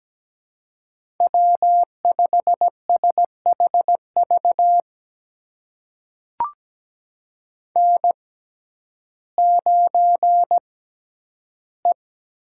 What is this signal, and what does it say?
Morse code, heard as a keyed tone, about 17 wpm, spaced as for 10 wpm, 700 Hz. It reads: W5SHV N9E
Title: USA Call Signs - 5 Characters - 6x Character Spacing - 17wpm